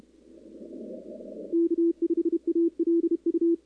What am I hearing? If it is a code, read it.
Morse code, heard as a keyed tone, about 30 wpm, 340 Hz. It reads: K5ALU
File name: contests / 2019-cw-wpx-kt5j-k5pi- -2330 0-10